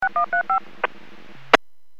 DTMF, heard as telephone touch tones, tones of about 110 ms. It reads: 6135
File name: WPS / 00124 13-02 WPS 451.25 WPS